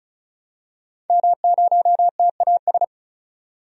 Morse code, heard as a keyed tone, about 36 wpm, 700 Hz. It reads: M0TAS
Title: DXCC-100 Call Signs - 5 Characters - 35wpm